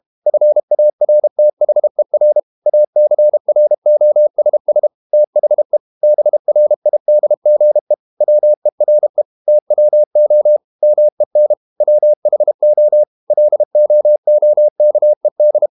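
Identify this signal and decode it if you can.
Morse code, heard as a keyed tone, about 32 wpm, 600 Hz. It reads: FARTHER ACROSS THE BRIDGE WERE TWO MEN WHO LOOKED
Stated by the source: Morse code practice at 32 wpm from CW med Gnister, reading Ken Follett's novel Never in English.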